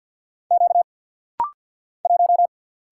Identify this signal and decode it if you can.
Morse code, heard as a keyed tone, about 50 wpm, 700 Hz. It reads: Q 1
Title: Single Letter-Number - Rapid-Fire 50wpm